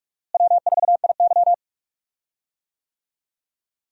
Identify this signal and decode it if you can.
Morse code, heard as a keyed tone, about 45 wpm, 700 Hz. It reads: W4IY
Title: USA Call Signs - 4 Characters - Repeated 3x - 45wpm